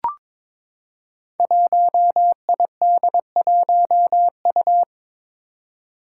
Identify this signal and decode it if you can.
Morse code, heard as a keyed tone, about 22 wpm, 700 Hz. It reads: 1ID1U
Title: Five Challenging Letter-Number Combinations 22wpm